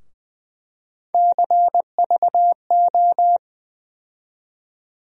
Morse code, heard as a keyed tone, about 20 wpm, 700 Hz. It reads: CVO
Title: USA Call Sign Postfixes - 3 Letters 20wpm